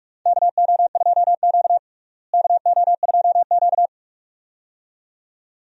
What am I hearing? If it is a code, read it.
Morse code, heard as a keyed tone, about 41 wpm, 700 Hz. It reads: KO2Q KO2Q
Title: USA Call Signs - 4 Characters - Repeated 2x - 45wpm